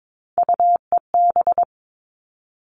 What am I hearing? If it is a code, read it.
Morse code, heard as a keyed tone, about 22 wpm, 700 Hz. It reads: UEB
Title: All Cut Numbers - 3-Digit 22wpm